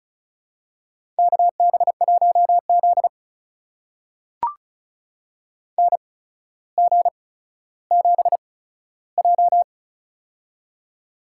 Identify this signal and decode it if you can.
Morse code, heard as a keyed tone, about 36 wpm, spaced as for 15 wpm, 700 Hz. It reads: KB1Z NG7J
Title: USA Call Signs - 8x Character Spacing - 35wpm